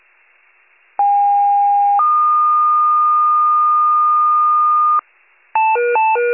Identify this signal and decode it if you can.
Two-tone page, heard as a one-second tone, then a three-second tone, about 810 then 1220 Hz. It is heard from 1.0 to 5.0 s.
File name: pager / 73 RESCUE / wcalert